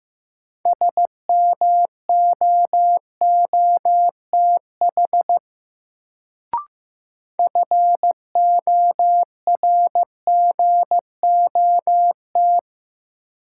Morse code, heard as a keyed tone, about 16 wpm, 700 Hz. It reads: SMOOTH FORGOT
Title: Sentences from Top 1400 Words - New Word Drill - 15wpm